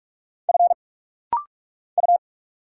Morse code, heard as a keyed tone, about 46 wpm, 700 Hz. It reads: F U
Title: Single Letters - Rapid-Fire 45wpm